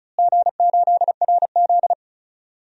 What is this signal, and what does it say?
Morse code, heard as a keyed tone, about 36 wpm, 700 Hz. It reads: G8RZ
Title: DXCC-100 Call Signs - 4 Characters - Repeated 3x - 35wpm